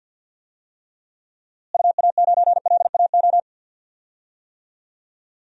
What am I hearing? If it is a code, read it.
Morse code, heard as a keyed tone, about 52 wpm, 700 Hz. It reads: UA9LAO
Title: DXCC-100 Call Signs - 6 Characters - Repeated 2x - 50wpm